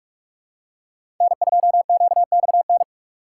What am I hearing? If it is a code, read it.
Morse code, heard as a keyed tone, about 41 wpm, 700 Hz. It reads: N2QXN
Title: USA Call Signs - 5 Characters - 6x Character Spacing - 45wpm